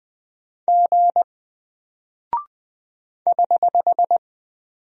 Morse code, heard as a keyed tone, about 20 wpm, 700 Hz. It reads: G <HH>
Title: Single Character (Letter:Number:Punctuation:Prosign) Alt 20wpm